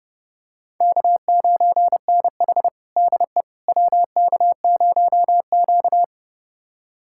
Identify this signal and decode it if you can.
Morse code, heard as a keyed tone, about 30 wpm, 700 Hz. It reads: K9NH DE WK0Q
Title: USA Callsign DE Callsign - 4 Characters - 30wpm